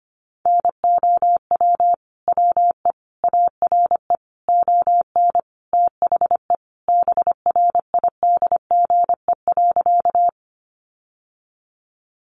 Morse code, heard as a keyed tone, about 25 wpm, 700 Hz. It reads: NOW WE ARE ON THE BRIDGE.
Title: Sentences from Top 1200 Words 25wpm